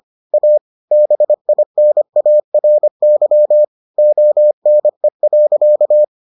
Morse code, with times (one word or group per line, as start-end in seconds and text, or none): 0.3-0.6 A
0.9-3.6 BINARY
4.0-6.0 ONE.